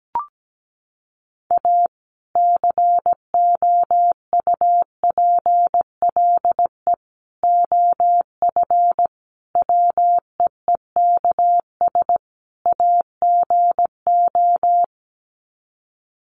Morse code, heard as a keyed tone, about 17 wpm, 700 Hz. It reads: A COUPLE OF WEEKS AGO
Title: Sets of 5 Words 17wpm